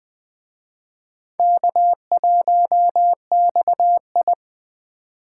Morse code, heard as a keyed tone, about 21 wpm, 700 Hz. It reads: K1XI